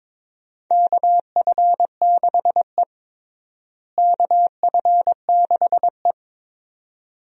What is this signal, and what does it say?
Morse code, heard as a keyed tone, about 22 wpm, 700 Hz. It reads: KF6E KF6E
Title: USA Call Signs - 4 Characters - Repeated 2x - 22wpm